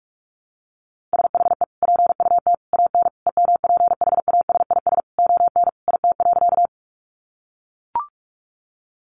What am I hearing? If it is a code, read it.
Morse code, heard as a keyed tone, about 44 wpm, 700 Hz. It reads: SHE PUT AN EMPHASIS ON IT.
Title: Sentences from Top 1400 Words 45wpm